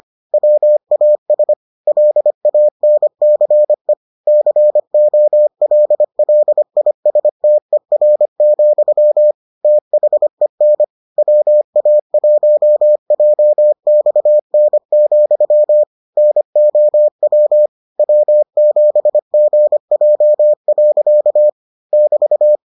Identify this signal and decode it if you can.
Morse code, heard as a keyed tone, about 25 wpm, 600 Hz. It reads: WAS LANCE COLLISTER, THEN WA1JXN, NOW W7GJ. <BT>